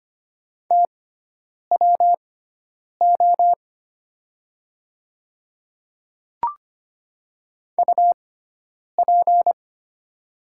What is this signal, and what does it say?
Morse code, heard as a keyed tone, about 25 wpm, spaced as for 9 wpm, 700 Hz. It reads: TWO UP
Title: 6x Character Spacing - Top 200 Words - 25wpm